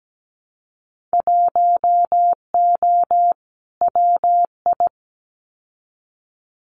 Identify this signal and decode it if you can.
Morse code, heard as a keyed tone, about 17 wpm, 700 Hz. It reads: 1O WI
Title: Winter Field Day Exchanges 17wpm